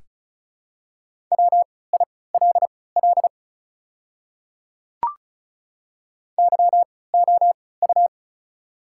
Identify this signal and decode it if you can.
Morse code, heard as a keyed tone, about 36 wpm, spaced as for 20 wpm, 690 Hz. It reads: WILL YOU
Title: 3x Character Spacing - Top 100 Words - 35wpm